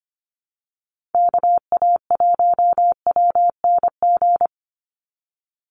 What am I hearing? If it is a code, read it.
Morse code, heard as a keyed tone, about 25 wpm, 700 Hz. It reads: KA1WNG
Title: USA Call Signs 25wpm